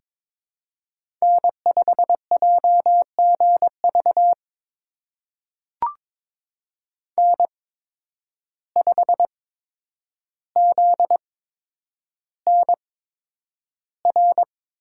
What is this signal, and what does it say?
Morse code, heard as a keyed tone, about 22 wpm, spaced as for 9 wpm, 700 Hz. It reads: N5JGV N5ZNR
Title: USA Call Signs - 5 Characters - 8x Character Spacing - 22wpm